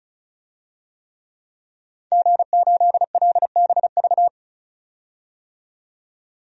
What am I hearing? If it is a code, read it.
Morse code, heard as a keyed tone, about 34 wpm, 700 Hz. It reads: G8LBV
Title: DXCC-100 Call Signs - 5 Characters - 35wpm